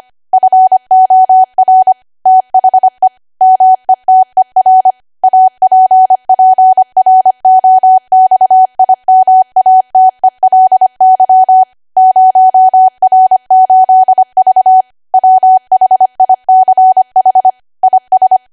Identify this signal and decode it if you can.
Morse code, heard as a keyed tone, about 25 wpm, 750 Hz. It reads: FOR THE METER APPROXIMATELY 0R8V WHICH IS
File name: Archive / 25 WPM / 25WPM